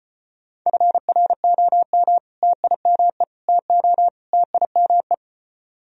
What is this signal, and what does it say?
Morse code, heard as a keyed tone, about 34 wpm, 700 Hz. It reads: FROM TIME TO TIME